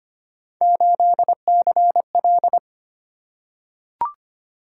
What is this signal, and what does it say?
Morse code, heard as a keyed tone, about 25 wpm, 700 Hz. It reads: 8CL
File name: Three Letter-Number Combinations - 25wpm